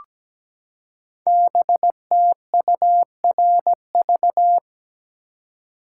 Morse code, heard as a keyed tone, about 17 wpm, 700 Hz. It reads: BTURV
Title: Code Groups - 5 Letters - 17wpm